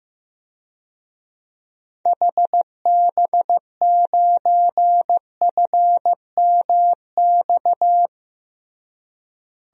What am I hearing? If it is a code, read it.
Morse code, heard as a keyed tone, about 15 wpm, 700 Hz. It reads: HB9FMX
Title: DXCC-100 Call Signs - 6 Characters - Repeated 2x - 15wpm